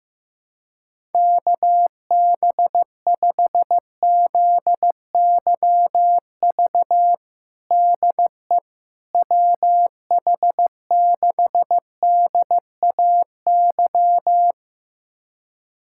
Morse code, heard as a keyed tone, about 15 wpm, 700 Hz. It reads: KB5ZYV DE WH6DAY
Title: USA Callsign DE Callsign - 6 Characters - 15wpm